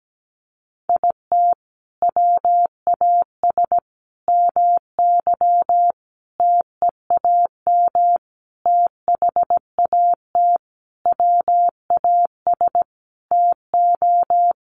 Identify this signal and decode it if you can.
Morse code, heard as a keyed tone, about 17 wpm, 700 Hz. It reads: IT WAS MY TEAM THAT WAS TO
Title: Sentences from Top 1500 Words 17wpm